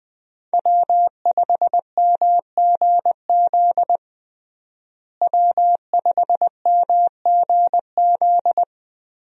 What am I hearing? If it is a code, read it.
Morse code, heard as a keyed tone, about 20 wpm, 700 Hz. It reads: W5MGZ W5MGZ
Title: USA Call Signs - 5 Characters - Repeated 2x - 20wpm